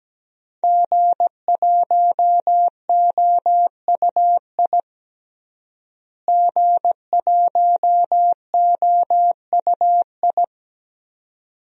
Morse code, heard as a keyed tone, about 17 wpm, 700 Hz. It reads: G1OUI G1OUI
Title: DXCC-100 Call Signs - 5 Characters - Repeated 2x - 17wpm